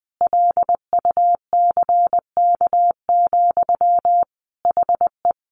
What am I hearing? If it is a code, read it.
Morse code, heard as a keyed tone, about 20 wpm, 700 Hz. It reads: LUCK, HE
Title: Sentences from Top 1500 Words 20wpm